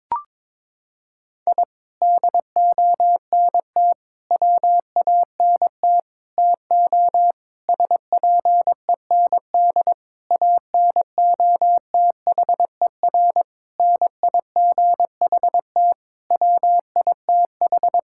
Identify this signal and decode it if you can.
Morse code, heard as a keyed tone, about 22 wpm, 700 Hz. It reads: I DONT WANT TO SPEND ANOTHER NIGHT WITH
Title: Sentences from Top 1000 Words - Review Part 5 - 22wpm